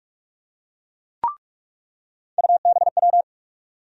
Morse code, heard as a keyed tone, about 47 wpm, 700 Hz. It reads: UBW
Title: USA Call Sign Postfixes - 3 Letters 45wpm